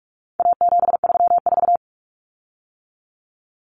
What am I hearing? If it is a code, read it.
Morse code, heard as a keyed tone, about 44 wpm, 700 Hz. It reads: A734
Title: Common Cut Numbers - 4-Digit 45wpm